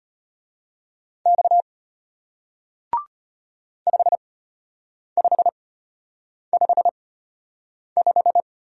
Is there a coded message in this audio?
Yes: Morse code from 1.3 to 8.4 s, heard as a keyed tone, about 33 wpm, spaced as for 8 wpm, 700 Hz. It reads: X 5555